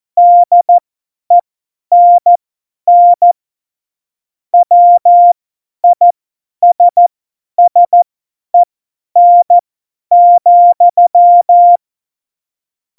Morse code, heard as a keyed tone, about 14 wpm, spaced as for 10 wpm, 700 Hz. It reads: DENN WISSEN,
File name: Morsekurs / Witze / Witz 14 Juli 2020, 50 F70